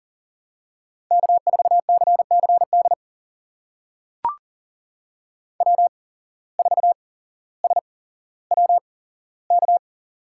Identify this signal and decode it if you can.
Morse code, heard as a keyed tone, about 38 wpm, spaced as for 17 wpm, 700 Hz. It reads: K4CCD W4SWK
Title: USA Call Signs - 5 Characters - 8x Character Spacing - 40wpm